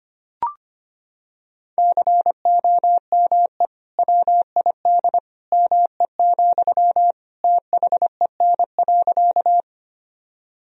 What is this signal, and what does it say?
Morse code, heard as a keyed tone, about 25 wpm, 700 Hz. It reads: COME WID ME, THEN.
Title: Sentences from Top 200 Words and Abbreviations 25wpm